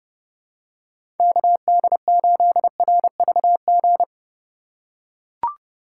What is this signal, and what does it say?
Morse code, heard as a keyed tone, about 30 wpm, 700 Hz. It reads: KD8RVG
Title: USA Call Signs - 6 Characters - 4x Character Spacing - 30wpm